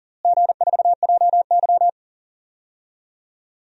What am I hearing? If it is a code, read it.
Morse code, heard as a keyed tone, about 40 wpm, 700 Hz. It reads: G4JY
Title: DXCC-100 Call Signs - 4 Characters - Repeated 3x - 40wpm